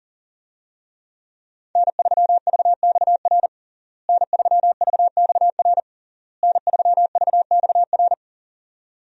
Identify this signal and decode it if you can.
Morse code, heard as a keyed tone, about 40 wpm, 700 Hz. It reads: N3VXR N3VXR N3VXR
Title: USA Call Signs - 5 Characters - Repeated 3x - 40wpm